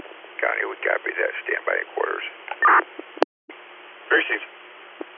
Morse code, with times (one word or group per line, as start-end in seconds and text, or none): none